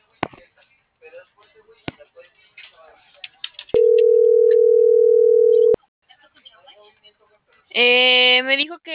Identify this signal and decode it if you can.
Ringback tone (North America), 440 + 480 Hz.